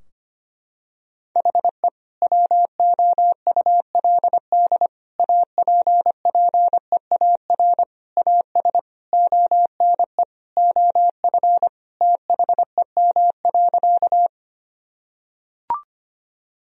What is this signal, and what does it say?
Morse code, heard as a keyed tone, about 25 wpm, 700 Hz. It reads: HE WOULD APPEAR AS ONE OF THEM.
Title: Sentences from Top 1000 Words - Review Part 2 - 25wpm